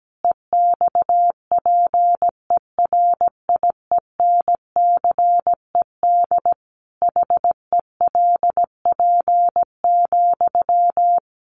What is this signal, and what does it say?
Morse code, heard as a keyed tone, about 17 wpm, 700 Hz. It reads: EXPERIENCED HELP,
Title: Sentences from Top 1100 Words 17wpm